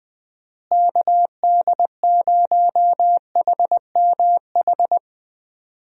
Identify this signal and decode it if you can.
Morse code, heard as a keyed tone, about 20 wpm, 700 Hz. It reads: KD0HMH
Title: USA Call Signs - 6 Characters - Repeated 2x - 20wpm